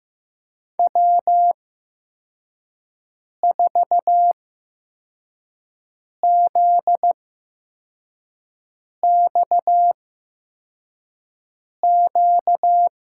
Morse code, heard as a keyed tone, about 15 wpm, spaced as for 4 wpm, 700 Hz. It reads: W4ZXQ
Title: USA Call Signs - 5 Characters - 8x Character Spacing - 15wpm